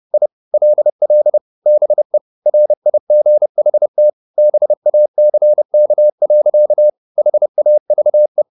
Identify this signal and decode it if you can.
Morse code, heard as a keyed tone, about 30 wpm, 600 Hz. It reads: I LL BE RIGHT BACK. HAVE
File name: the old man and the sea / the old man and the sea 1 30wpm